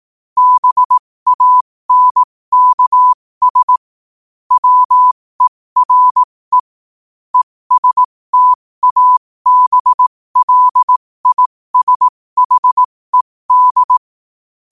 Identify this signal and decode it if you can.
Morse code, heard as a keyed tone, about 18 wpm, spaced as for 15 wpm, 1000 Hz. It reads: BANKS WERE ESTABLISHED
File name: qotdfiles / 15wpm